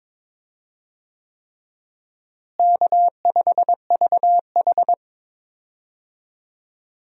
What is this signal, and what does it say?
Morse code, heard as a keyed tone, about 22 wpm, 700 Hz. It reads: K5VH
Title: USA Call Signs - 4 Characters - Repeated 3x - 22wpm